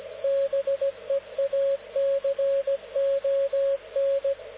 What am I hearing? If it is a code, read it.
Morse code, heard as a keyed tone, about 17 wpm, 550 Hz, with radio noise. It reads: BEACON
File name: luxorion / Radio / dk0wcy-spaceweather-10144khz